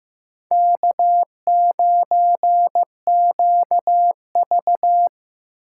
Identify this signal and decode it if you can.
Morse code, heard as a keyed tone, about 18 wpm, 700 Hz. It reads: K9QV
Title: Speed Racing - USA Call Signs 15wpm